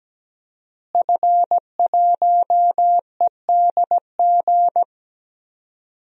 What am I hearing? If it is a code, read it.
Morse code, heard as a keyed tone, about 17 wpm, 700 Hz. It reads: F1EDG
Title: DXCC-100 Call Signs - 5 Characters - Repeated 2x - 17wpm